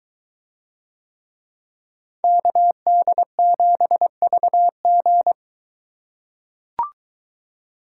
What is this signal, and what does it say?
Morse code, heard as a keyed tone, about 23 wpm, 700 Hz. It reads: KD7VG